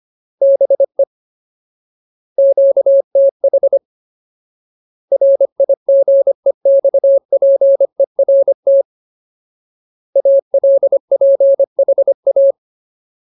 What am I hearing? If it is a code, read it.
Morse code, heard as a keyed tone, about 25 wpm, 550 Hz. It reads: BE QTH RIGEXPERT ALPHA
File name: Ultimate Word List / 25WPM Ultimate